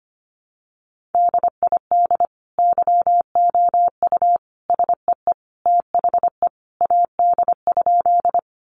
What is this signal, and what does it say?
Morse code, heard as a keyed tone, about 25 wpm, 700 Hz. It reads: DID YOU SEE THE AD?
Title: Sentences from Top 1300 Words 25wpm